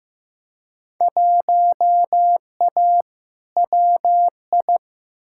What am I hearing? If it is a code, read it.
Morse code, heard as a keyed tone, about 15 wpm, 700 Hz. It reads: 1A WI
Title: Field Day Exchanges - 15wpm